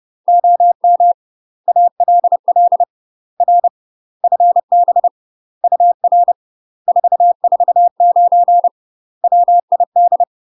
Morse code, heard as a keyed tone, about 30 wpm, 700 Hz. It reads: OM ALL R FB UR 449 WID
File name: UK MASTER QSOs TRAINING / 30WPM / UK MASTER QSOs 0034